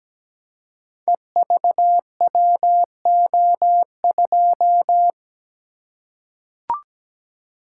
Morse code, heard as a keyed tone, about 17 wpm, 700 Hz. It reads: EVWO2
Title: Code Groups - 5 Letter:Numbers - 17wpm